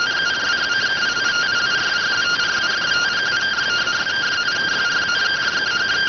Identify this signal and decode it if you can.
Two-tone teleprinter signal, about 170 Hz shift.